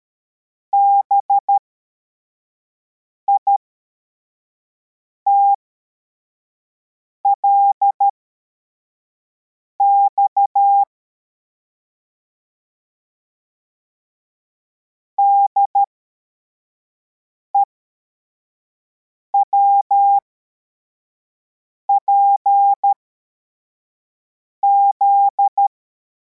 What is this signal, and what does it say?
Morse code, heard as a keyed tone, about 13 wpm, spaced as for 4 wpm, 800 Hz. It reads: BITLX DEWPZ
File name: formation / cw / Text84